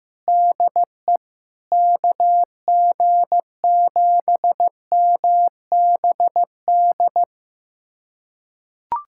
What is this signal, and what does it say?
Morse code, heard as a keyed tone, about 15 wpm, 700 Hz. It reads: DE KG7MBD
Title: USA Callsign DE Callsign - 6 Characters - 15wpm